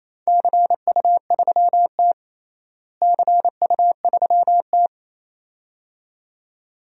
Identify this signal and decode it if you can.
Morse code, heard as a keyed tone, about 28 wpm, 700 Hz. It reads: CU3T CU3T
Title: DXCC-100 Call Signs - 4 Characters - Repeated 2x - 28wpm